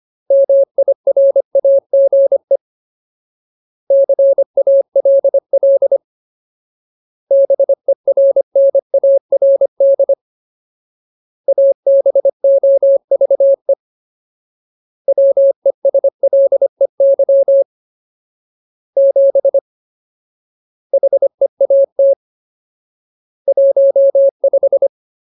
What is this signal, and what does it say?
Morse code, heard as a keyed tone, about 25 wpm, 550 Hz. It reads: MIRAGE CALL BERNARD ABOVE WESLEY 7 HEAT 15
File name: Ultimate Word List / 25WPM Ultimate